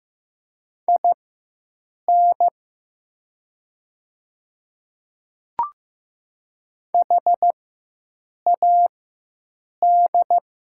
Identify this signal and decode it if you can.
Morse code, heard as a keyed tone, about 15 wpm, spaced as for 7 wpm, 700 Hz. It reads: IN HAD
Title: 4x Character Spacing - Top 200 Words - 15wpm